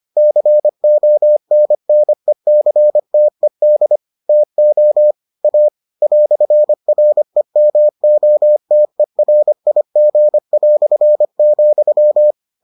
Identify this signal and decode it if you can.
Morse code, heard as a keyed tone, about 25 wpm, 600 Hz. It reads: CONNECTED TO A "REMOTERIG",